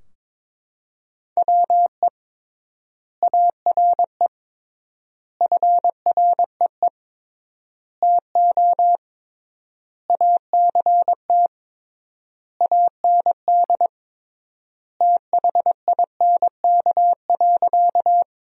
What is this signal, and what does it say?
Morse code, heard as a keyed tone, about 22 wpm, 700 Hz. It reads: WE ARE FREE TO ACT AND THINK.
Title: 3x Word-Spacing - Sentences from Top 500 Words - 22wpm